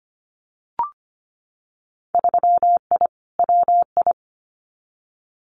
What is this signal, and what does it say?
Morse code, heard as a keyed tone, about 25 wpm, 700 Hz. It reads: 3I WI
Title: Winter Field Day Exchanges 25wpm